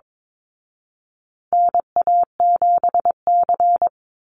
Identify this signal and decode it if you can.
Morse code, heard as a keyed tone, about 22 wpm, 700 Hz. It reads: NA7C
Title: Call Signs Repeated 2x - 22wpm